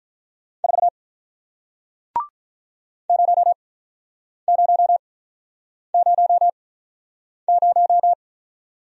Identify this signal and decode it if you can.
Morse code, heard as a keyed tone, about 47 wpm, spaced as for 9 wpm, 700 Hz. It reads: 4 0000